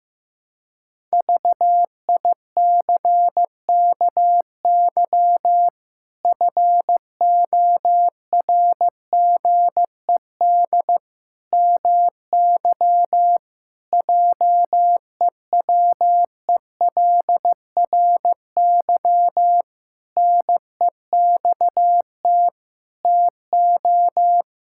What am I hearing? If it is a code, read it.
Morse code, heard as a keyed tone, about 16 wpm, 700 Hz. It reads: VICKY FORGED MY JEWELRY NEXT TO
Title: Pangram Challenge v2 - 15wpm